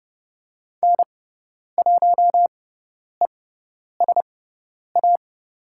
Morse code, heard as a keyed tone, about 30 wpm, spaced as for 10 wpm, 700 Hz. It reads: N1ESA
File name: Lesson 18g - Letter m - Call Signs 30wpm